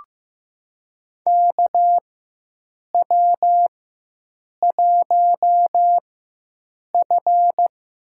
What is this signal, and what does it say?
Morse code, heard as a keyed tone, about 15 wpm, spaced as for 7 wpm, 700 Hz. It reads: KW1F
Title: USA Call Signs - 4 Characters - 4x Character Spacing - 15wpm